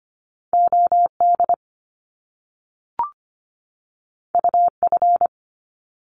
Morse code, heard as a keyed tone, about 25 wpm, 700 Hz. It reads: O D U F
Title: Top Two Letter Combinations in Words 25wpm